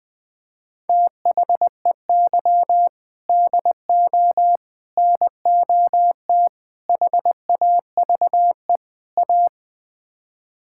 Morse code, heard as a keyed tone, about 21 wpm, 700 Hz. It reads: THEY DO NOT HAVE A
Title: Build A Phrase - 5-Word Phrases - 20wpm